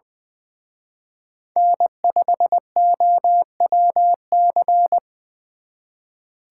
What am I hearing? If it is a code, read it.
Morse code, heard as a keyed tone, about 23 wpm, 700 Hz. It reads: N5OWC
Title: Speed Racing - USA 1x3 Call Signs 20wpm